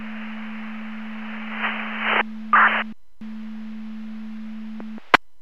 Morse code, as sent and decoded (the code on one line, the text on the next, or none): none